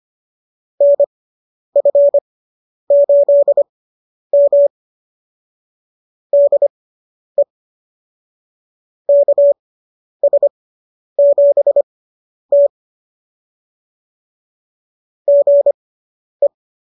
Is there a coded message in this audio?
Yes: Morse code from 0.8 to 16.5 s, heard as a keyed tone, about 25 wpm, spaced as for 10 wpm, 590 Hz. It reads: NF8M DE KS7T GE